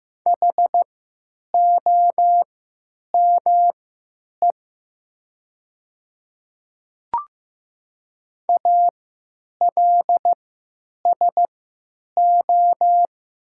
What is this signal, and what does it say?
Morse code, heard as a keyed tone, about 15 wpm, spaced as for 9 wpm, 700 Hz. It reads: HOME ALSO
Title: 3x Character Spacing - Top 200 Words - 15wpm